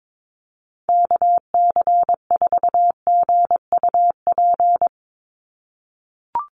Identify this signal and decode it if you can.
Morse code, heard as a keyed tone, about 22 wpm, 700 Hz. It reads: KC4GUP
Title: USA Call Signs 22wpm